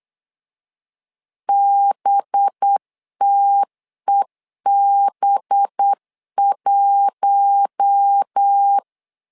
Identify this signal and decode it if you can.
Morse code, heard as a keyed tone, about 8 wpm, 800 Hz. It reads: BTEB1
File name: LEZIONI / 5 LBVC / 5 LB 8wpm 2816